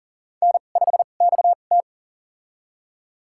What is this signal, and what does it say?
Morse code, heard as a keyed tone, about 40 wpm, spaced as for 29 wpm, 700 Hz. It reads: N5XT
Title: USA Call Signs - 4 Characters - 2x Character Spacing - 40wpm